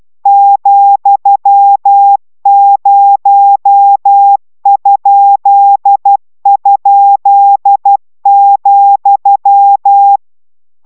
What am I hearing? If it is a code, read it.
Morse code, heard as a keyed tone, about 12 wpm, 800 Hz. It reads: ,0??,